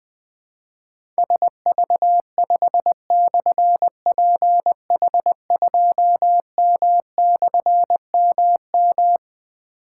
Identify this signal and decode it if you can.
Morse code, heard as a keyed tone, about 20 wpm, 700 Hz. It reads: SV5/PH2M/MM
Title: Difficult Call Signs 20wpm